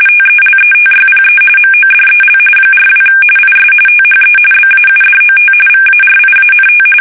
RTTY, about 45.45 baud. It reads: TELETYPE MODEL 42 IS THE BAUDOT VARIANT
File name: fsk850 / teletype-corp